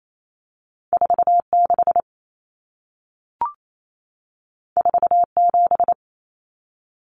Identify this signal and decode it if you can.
Morse code, heard as a keyed tone, about 28 wpm, 700 Hz. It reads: 46 47
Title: Numbers - 2-Digit 28wpm